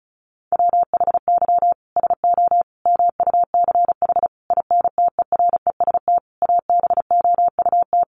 Morse code, heard as a keyed tone, about 34 wpm, 700 Hz. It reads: WHY SO MUCH INTEREST ABOUT
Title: Sentences from Top 1200 Words 35wpm